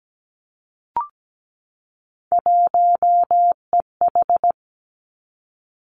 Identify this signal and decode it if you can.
Morse code, heard as a keyed tone, about 17 wpm, 700 Hz. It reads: 1EH
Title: Three Challenging Letter-Number Combinations 17wpm